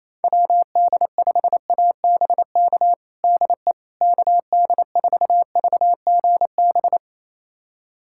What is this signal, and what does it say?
Morse code, heard as a keyed tone, about 28 wpm, 700 Hz. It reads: WD5ABK DE KD4VGB
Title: USA Callsign DE Callsign 28wpm